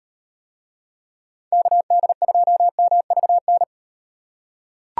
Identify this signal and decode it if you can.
Morse code, heard as a keyed tone, about 38 wpm, 700 Hz. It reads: KD2MVN